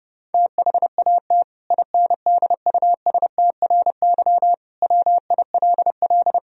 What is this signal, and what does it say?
Morse code, heard as a keyed tone, about 30 wpm, 700 Hz. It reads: THAT INDUSTRY WILL
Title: Sentences from Top 1000 Words - Review Part 4 - 30wpm